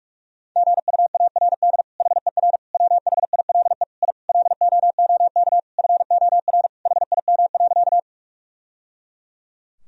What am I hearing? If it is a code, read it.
Morse code, heard as a keyed tone, about 46 wpm, 700 Hz. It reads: GUARD HER WHILE I LOOK FOR HIM.
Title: Sentences from Top 1000 Words - Review Part 4 - 45wpm